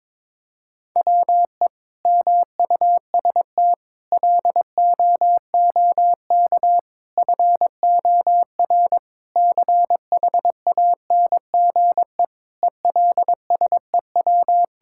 Morse code, heard as a keyed tone, about 22 wpm, 700 Hz. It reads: WE MUST LOOK FOR CHANGE ELSEW
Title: Sentences from Top 1600 Words 22wpm